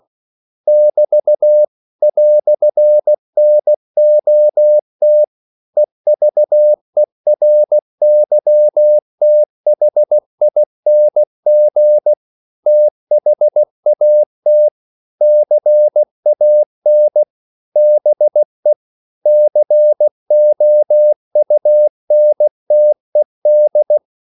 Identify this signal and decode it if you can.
Morse code, heard as a keyed tone, about 16 wpm, 600 Hz. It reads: <BT> "NOT EVERYTHING THAT CAN BE COUNTED
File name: CW / Practice / EnglishQuotations0116